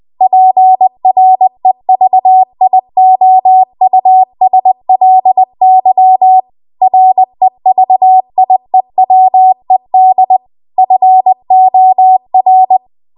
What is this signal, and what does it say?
Morse code, heard as a keyed tone, about 20 wpm, 750 Hz. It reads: PREVIOUSLY REVIEWED FOR